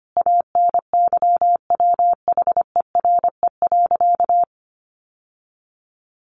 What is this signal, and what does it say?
Morse code, heard as a keyed tone, about 25 wpm, 700 Hz. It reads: ANYWHERE.
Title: Sentences from Top 1400 Words 25wpm